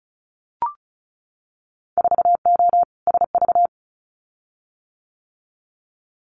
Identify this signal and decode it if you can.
Morse code, heard as a keyed tone, about 34 wpm, 700 Hz. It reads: 4O SV
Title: Winter Field Day Exchanges Encore 35wpm